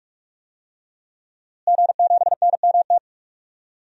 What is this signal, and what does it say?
Morse code, heard as a keyed tone, about 41 wpm, 700 Hz. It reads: G7NMT
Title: DXCC-100 Call Signs - 5 Characters - Repeated 3x - 45wpm